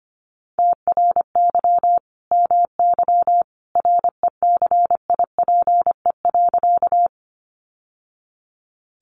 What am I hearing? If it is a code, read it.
Morse code, heard as a keyed tone, about 25 wpm, 700 Hz. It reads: TRY MY RECIPE.
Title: Sentences from Top 1200 Words 25wpm